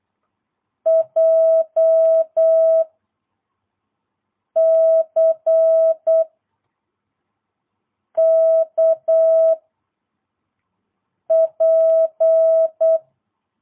Morse code, heard as a keyed tone, about 8 wpm, 660 Hz. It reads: J C K P
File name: jh8cbh / 2022cw